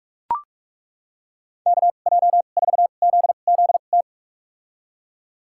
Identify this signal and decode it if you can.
Morse code, heard as a keyed tone, about 42 wpm, spaced as for 32 wpm, 700 Hz. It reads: KJ4ZZT
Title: USA Call Signs - 2x Character Spacing - 45wpm